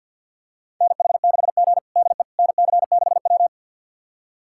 Morse code, heard as a keyed tone, about 52 wpm, 690 Hz. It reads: NH6G DE NC6W
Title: USA Callsign DE Callsign - 4 Characters - 50wpm